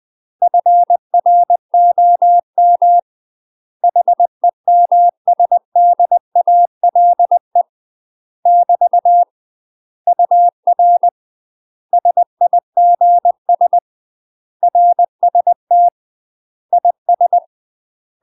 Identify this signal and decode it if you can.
Morse code, heard as a keyed tone, about 20 wpm, 700 Hz. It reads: FROM HEMSDALE <BT> UR SIGS RST IS
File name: UK MASTER QSOs TRAINING/20WPM